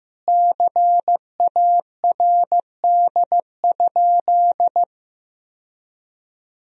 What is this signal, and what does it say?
Morse code, heard as a keyed tone, about 15 wpm, 700 Hz. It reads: CARD?
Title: Sentences from Top 1000 Words - Review Part 4 - 15wpm